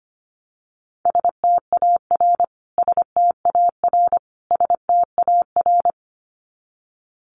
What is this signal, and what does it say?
Morse code, heard as a keyed tone, about 25 wpm, 700 Hz. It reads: STAR STAR STAR
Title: Top 500 Words Repeated 25wpm